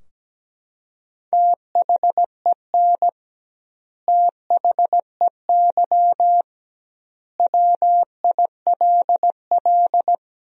Morse code, heard as a keyed tone, about 17 wpm, 700 Hz. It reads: THEN THEY WILL
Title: 2x Word-Spacing - Sentences from Top 200 Words - 17wpm